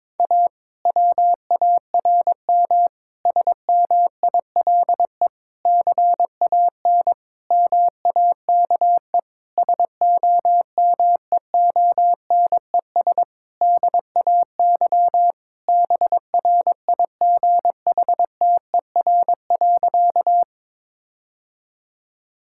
Morse code, heard as a keyed tone, about 22 wpm, 700 Hz. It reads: A WARM SMILE CAN MAKE SOMEONES DAY BRIGHTER.
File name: Difficult Sentences with Hints - 22wpm